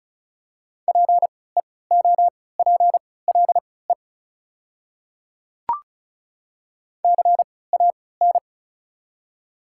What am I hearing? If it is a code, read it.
Morse code, heard as a keyed tone, about 36 wpm, spaced as for 20 wpm, 700 Hz. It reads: PEOPLE CAN
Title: 3x Character Spacing - Top 100 Words - 35wpm